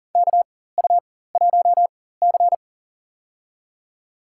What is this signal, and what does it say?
Morse code, heard as a keyed tone, about 37 wpm, spaced as for 18 wpm, 700 Hz. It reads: KU1C
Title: USA Call Signs - 4 Characters - 4x Character Spacing - 40wpm